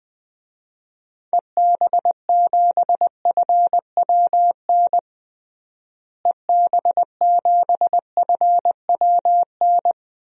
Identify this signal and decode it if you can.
Morse code, heard as a keyed tone, about 20 wpm, 700 Hz. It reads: EB7FWN EB7FWN
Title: DXCC-100 Call Signs - 6 Characters - Repeated 2x - 20wpm